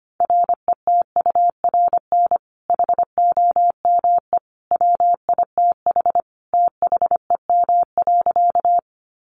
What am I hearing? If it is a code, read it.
Morse code, heard as a keyed tone, about 25 wpm, 700 Hz. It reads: RETURN HOME WITH THEM.
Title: Sentences from Top 800 Words 25wpm